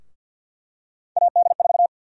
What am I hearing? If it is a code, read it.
Morse code, heard as a keyed tone, about 52 wpm, 700 Hz. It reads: AD4
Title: USA Call Sign Prefixes - Repeated 2x - 50wpm